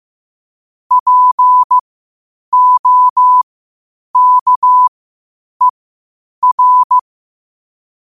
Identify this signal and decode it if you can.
Morse code, heard as a keyed tone, about 15 wpm, spaced as for 8 wpm, 1000 Hz. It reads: POKER
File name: qotdfiles / 8wpm